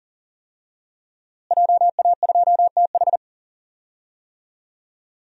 Morse code, heard as a keyed tone, about 40 wpm, 700 Hz. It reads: JA2TH
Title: DXCC-100 Call Signs - 5 Characters - Repeated 3x - 40wpm